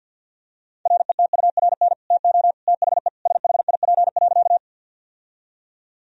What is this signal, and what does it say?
Morse code, heard as a keyed tone, about 50 wpm, 700 Hz. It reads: RETURN TO THE SHIP.
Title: Sentences from Top 1000 Words - Review Part 2 - 50wpm